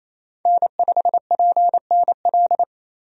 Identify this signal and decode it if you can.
Morse code, heard as a keyed tone, about 28 wpm, 700 Hz. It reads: N5PNL
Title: USA Call Signs - 5 Characters - 28wpm